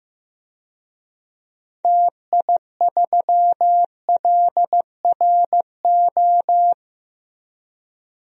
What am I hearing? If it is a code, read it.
Morse code, heard as a keyed tone, about 15 wpm, 700 Hz. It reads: TI3LRO
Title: DXCC-100 Call Signs - 6 Characters - Repeated 2x - 15wpm